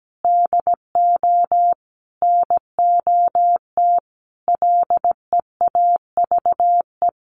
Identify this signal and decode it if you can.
Morse code, heard as a keyed tone, about 17 wpm, 700 Hz. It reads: DO NOT LEAVE
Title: Sentences from Top 1500 Words 17wpm